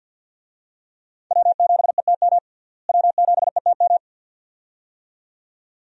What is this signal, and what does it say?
Morse code, heard as a keyed tone, about 52 wpm, 700 Hz. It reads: W7ETM W7ETM
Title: USA Call Signs - 5 Characters - Repeated 2x - 50wpm